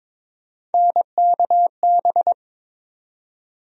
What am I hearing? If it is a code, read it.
Morse code, heard as a keyed tone, about 21 wpm, 700 Hz. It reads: NKB
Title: USA Call Sign Postfixes - 3 Letters 22wpm